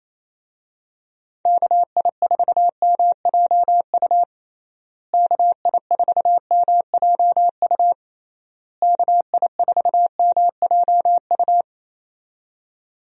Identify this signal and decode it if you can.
Morse code, heard as a keyed tone, about 28 wpm, 700 Hz. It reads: KI4MJU KI4MJU KI4MJU
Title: USA Call Signs - 6 Characters - Repeated 3x - 28wpm